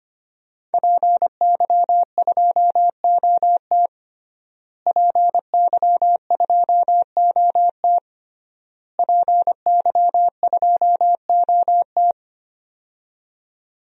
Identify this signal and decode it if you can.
Morse code, heard as a keyed tone, about 25 wpm, 700 Hz. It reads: PY2OT PY2OT PY2OT
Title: DXCC-100 Call Signs - 5 Characters - Repeated 3x - 25wpm